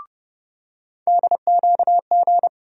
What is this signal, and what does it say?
Morse code, heard as a keyed tone, about 30 wpm, 700 Hz. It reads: DQG
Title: USA Call Sign Postfixes - 3 Letters 30wpm